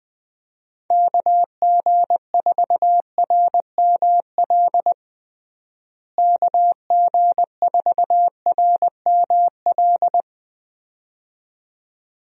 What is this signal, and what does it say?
Morse code, heard as a keyed tone, about 20 wpm, 700 Hz. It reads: KG4RML KG4RML
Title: USA Call Signs - 6 Characters - Repeated 2x - 20wpm